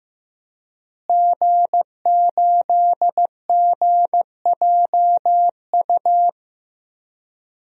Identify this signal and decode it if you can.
Morse code, heard as a keyed tone, about 14 wpm, 700 Hz. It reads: G8GJU
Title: DXCC-100 Call Signs - 5 Characters - Repeated 2x - 15wpm